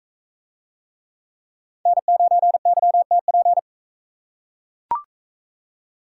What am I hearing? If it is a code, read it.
Morse code, heard as a keyed tone, about 37 wpm, 700 Hz. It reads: N9YTP